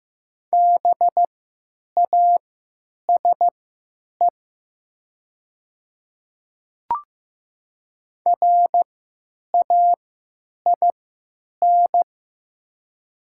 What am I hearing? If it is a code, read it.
Morse code, heard as a keyed tone, about 15 wpm, spaced as for 9 wpm, 700 Hz. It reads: BASE RAIN
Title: 3x Character Spacing - Top 500 Words - 15wpm